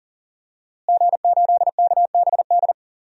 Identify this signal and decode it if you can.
Morse code, heard as a keyed tone, about 40 wpm, 700 Hz. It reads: G8KBD
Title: DXCC-100 Call Signs - 5 Characters - Repeated 2x - 40wpm